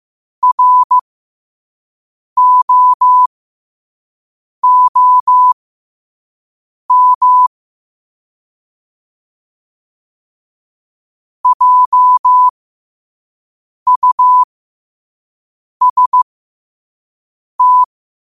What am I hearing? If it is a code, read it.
Morse code, heard as a keyed tone, about 15 wpm, spaced as for 5 wpm, 1000 Hz. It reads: ROOM JUST
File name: qotdfiles / 5wpm